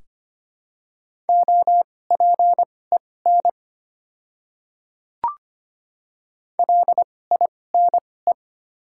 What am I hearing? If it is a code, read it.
Morse code, heard as a keyed tone, about 25 wpm, spaced as for 18 wpm, 690 Hz. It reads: OPEN LINE
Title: 2x Character Spacing - Top 300 Words - 25wpm